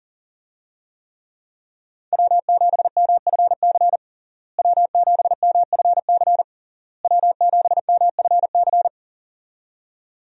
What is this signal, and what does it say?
Morse code, heard as a keyed tone, about 40 wpm, 690 Hz. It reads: W7MFC W7MFC W7MFC
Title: USA Call Signs - 5 Characters - Repeated 3x - 40wpm